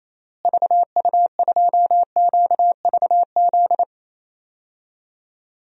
Morse code, heard as a keyed tone, about 28 wpm, 700 Hz. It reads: VU2QVZ
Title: DXCC-100 Call Signs - 6 Characters - Repeated 3x - 28wpm